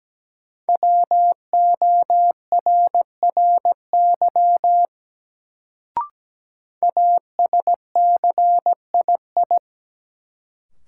Morse code, heard as a keyed tone, about 17 wpm, 700 Hz. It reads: WORRY ASCII
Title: Sets of 5 Words - New Word Drill - 17wpm